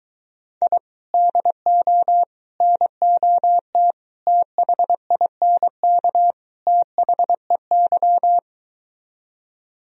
Morse code, heard as a keyed tone, about 23 wpm, 700 Hz. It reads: I DO NOT THINK THEY